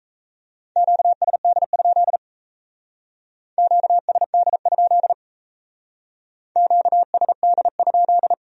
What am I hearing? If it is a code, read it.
Morse code, heard as a keyed tone, about 36 wpm, 700 Hz. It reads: QSD? QSD? QSD?